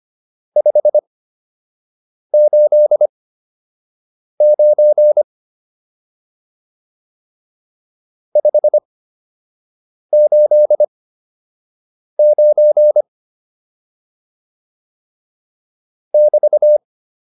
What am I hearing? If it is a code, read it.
Morse code, heard as a keyed tone, about 25 wpm, spaced as for 6 wpm, 600 Hz. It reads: 589 589 <BT>